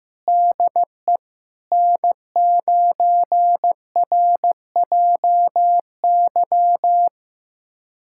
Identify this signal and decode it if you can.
Morse code, heard as a keyed tone, about 15 wpm, 700 Hz. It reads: DE N9RJY
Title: USA Callsign DE Callsign - 5 Characters - 15wpm